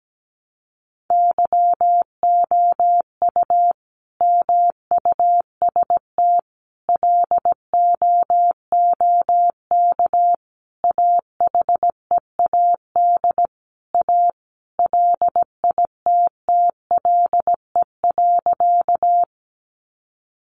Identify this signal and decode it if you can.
Morse code, heard as a keyed tone, about 17 wpm, 700 Hz. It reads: YOU MUST LOOK AHEAD A LITTLE.
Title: Sentences from Top 1100 Words 17wpm